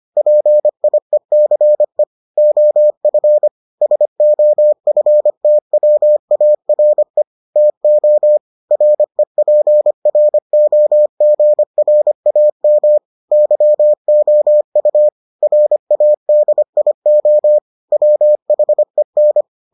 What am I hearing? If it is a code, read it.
Morse code, heard as a keyed tone, about 25 wpm, 600 Hz. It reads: PIECE OF SOFTWARE TO REPROGRAM YOU RADIO WHEN